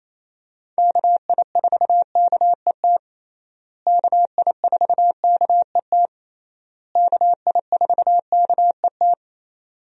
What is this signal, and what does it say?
Morse code, heard as a keyed tone, about 28 wpm, 700 Hz. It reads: KI4KET KI4KET KI4KET
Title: USA Call Signs - 6 Characters - Repeated 3x - 28wpm